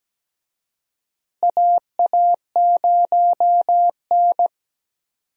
Morse code, heard as a keyed tone, about 17 wpm, 700 Hz. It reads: AA0N
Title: USA Call Signs - 4 Characters - Repeated 2x - 17wpm